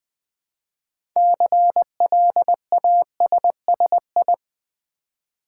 Morse code, heard as a keyed tone, about 20 wpm, 700 Hz. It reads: CLASSI
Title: Build A Word - 3 Letters At A Time - 9-Letter Words - 20wpm